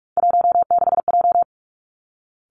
Morse code, heard as a keyed tone, about 46 wpm, 700 Hz. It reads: 16J
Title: Three Challenging Letter-Number Combinations 45wpm